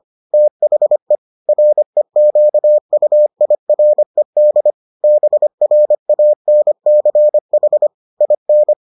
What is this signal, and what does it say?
Morse code, heard as a keyed tone, about 25 wpm, 600 Hz. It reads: THE REQUIRED BRANCH IN